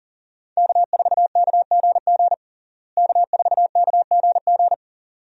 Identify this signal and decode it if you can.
Morse code, heard as a keyed tone, about 40 wpm, 700 Hz. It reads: K4KGG K4KGG
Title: USA Call Signs - 5 Characters - Repeated 2x - 40wpm